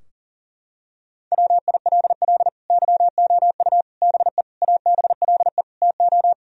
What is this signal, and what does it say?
Morse code, heard as a keyed tone, about 40 wpm, 700 Hz. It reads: WILL YOU BE ABLE TO
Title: Sentences from Top 1000 Words - Review Part 2 - 40wpm